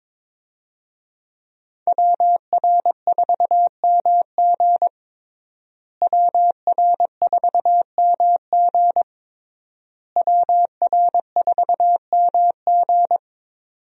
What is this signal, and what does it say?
Morse code, heard as a keyed tone, about 22 wpm, 700 Hz. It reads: WR4MG WR4MG WR4MG
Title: USA Call Signs - 5 Characters - Repeated 3x - 22wpm